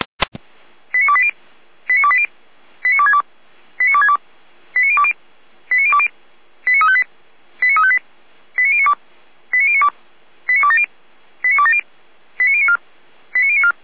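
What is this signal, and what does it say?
Five-tone selective call (ZVEI), heard as quick runs of five tones, tones of about 70 ms. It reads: 89280 89280 89262 89262 89020 89020 89358 89358 89092 89092 89270 89270 89094 89094
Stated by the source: Five-tone ZVEI selective call received on an SDR.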